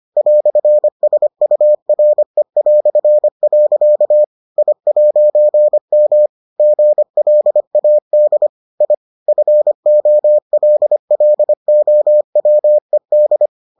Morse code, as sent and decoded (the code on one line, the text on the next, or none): .-..-. ... ..- .-. .-..-. .-.-.- / .. .----. -- / --. .-.. .- -.. / .. / ..-. --- .-.. .-.. --- .-- . -..
"SURE". I'M GLAD I FOLLOWED